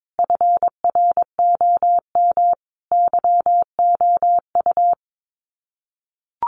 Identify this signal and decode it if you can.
Morse code, heard as a keyed tone, about 22 wpm, 700 Hz. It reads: FROM YOU
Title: Common QSO Phrases 22wpm